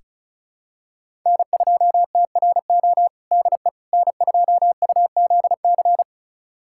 Morse code, heard as a keyed tone, about 34 wpm, 700 Hz. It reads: N2TRO DE N2UZC
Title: USA Callsign DE Callsign - 5 Characters - 35wpm